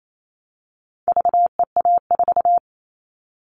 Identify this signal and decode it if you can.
Morse code, heard as a keyed tone, about 28 wpm, 700 Hz. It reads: VEA4
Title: All Cut Numbers - 4-Digit 28wpm